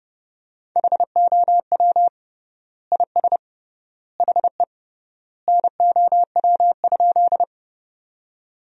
Morse code, heard as a keyed tone, about 32 wpm, 700 Hz. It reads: HOW IS HE NOW?
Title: 3x Word-Spacing - Sentences from Top 100 Words - 30wpm